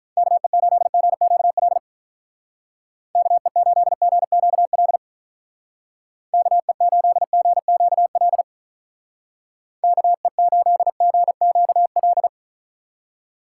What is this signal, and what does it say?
Morse code, heard as a keyed tone, about 43 wpm, 700 Hz. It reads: KE8GQL KE8GQL KE8GQL KE8GQL
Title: Speed Racing - USA Call Signs 35wpm